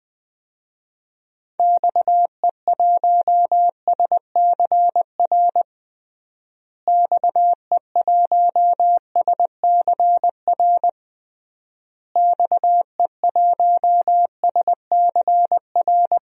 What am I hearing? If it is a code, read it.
Morse code, heard as a keyed tone, about 20 wpm, 700 Hz. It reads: XE1SCR XE1SCR XE1SCR
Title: DXCC-100 Call Signs - 6 Characters - Repeated 3x - 20wpm